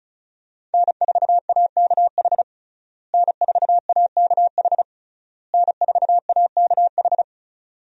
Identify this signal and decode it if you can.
Morse code, heard as a keyed tone, about 36 wpm, 700 Hz. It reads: N4AKH N4AKH N4AKH
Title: USA Call Signs - 5 Characters - Repeated 3x - 35wpm